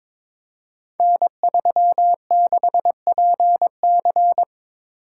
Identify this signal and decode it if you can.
Morse code, heard as a keyed tone, about 22 wpm, 700 Hz. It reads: N36PC
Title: Code Groups - 5 Letter:Numbers - 22wpm